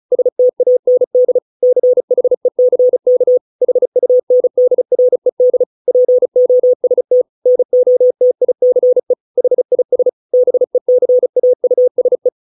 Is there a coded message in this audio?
Yes: Morse code from 0.1 to 12.3 s, heard as a keyed tone, about 34 wpm, 500 Hz. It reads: STAND CHECK HUNDRED POST NOTICE HIS BECAUSE